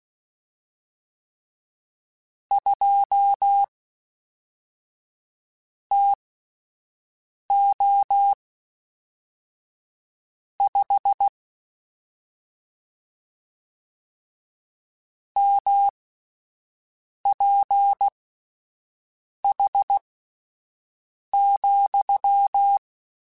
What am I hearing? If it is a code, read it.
Morse code, heard as a keyed tone, about 16 wpm, spaced as for 5 wpm, 800 Hz. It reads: 2 TO 5 MPH,
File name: Spaced Characters / Spaced Characters-mp3s / Weather5 12 sp @10